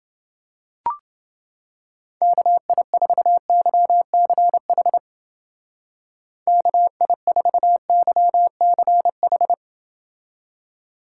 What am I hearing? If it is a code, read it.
Morse code, heard as a keyed tone, about 28 wpm, 700 Hz. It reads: KI4YCH KI4YCH